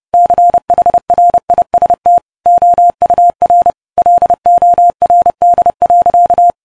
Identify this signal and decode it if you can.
Morse code, heard as a keyed tone, about 28 wpm, 700 Hz. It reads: CHRIST OUR LORD.